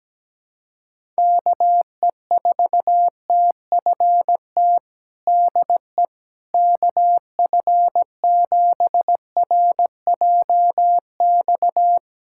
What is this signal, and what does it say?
Morse code, heard as a keyed tone, about 17 wpm, 700 Hz. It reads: KE4TFT DE KF7RJX
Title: USA Callsign DE Callsign - 6 Characters - 17wpm